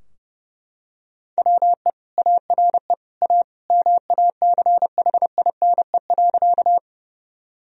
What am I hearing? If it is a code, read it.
Morse code, heard as a keyed tone, about 30 wpm, 700 Hz. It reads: WE ARE A MACHINE.
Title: Sentences from Top 1000 Words - Review Part 2 - 30wpm